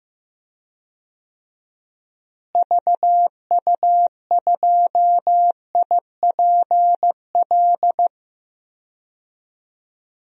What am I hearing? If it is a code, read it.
Morse code, heard as a keyed tone, about 15 wpm, 700 Hz. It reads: VU2IPL
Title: DXCC-100 Call Signs - 6 Characters - Repeated 3x - 15wpm